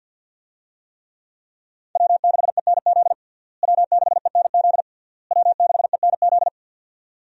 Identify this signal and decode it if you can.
Morse code, heard as a keyed tone, about 49 wpm, 690 Hz. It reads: W6ENZ W6ENZ W6ENZ
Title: USA Call Signs - 5 Characters - Repeated 3x - 50wpm